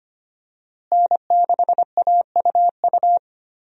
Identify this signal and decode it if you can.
Morse code, heard as a keyed tone, about 25 wpm, 700 Hz. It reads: N6AUU
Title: USA Call Signs - 5 Characters - Repeated 3x - 25wpm